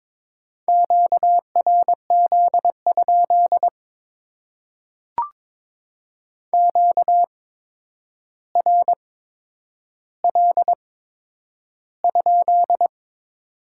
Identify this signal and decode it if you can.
Morse code, heard as a keyed tone, about 22 wpm, spaced as for 9 wpm, 690 Hz. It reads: QRZ? QRL?
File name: Top 90 QSO Elements - 8x Character Spacing - 22wpm